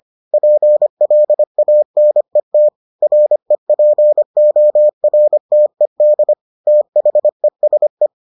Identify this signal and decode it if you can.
Morse code, heard as a keyed tone, about 25 wpm, 600 Hz. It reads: PLANET REPORTED THESE